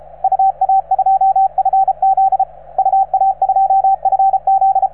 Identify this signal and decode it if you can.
Morse code, heard as a keyed tone, about 33 wpm, 770 Hz, with radio noise. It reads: UA2FZ UA2FZ